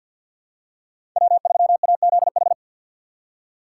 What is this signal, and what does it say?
Morse code, heard as a keyed tone, about 52 wpm, 700 Hz. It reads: W3AZH
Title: USA Call Signs - 2x Character Spacing - 50wpm